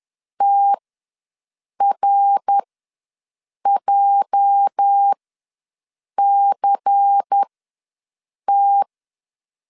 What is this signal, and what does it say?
Morse code, heard as a keyed tone, about 10 wpm, spaced as for 6 wpm, 800 Hz. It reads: TRJCT
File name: LEZIONI / 8 YJ45 / 8 YJ ws 10wpm 4043